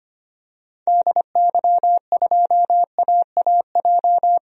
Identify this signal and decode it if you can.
Morse code, heard as a keyed tone, about 25 wpm, 700 Hz. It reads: DY2AAJ
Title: DXCC-100 Call Signs - 6 Characters - 25wpm